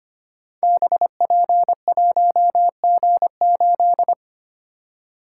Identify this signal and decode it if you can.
Morse code, heard as a keyed tone, about 25 wpm, 700 Hz. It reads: BP1G8
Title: Code Groups - 5 Letter:Numbers - 25wpm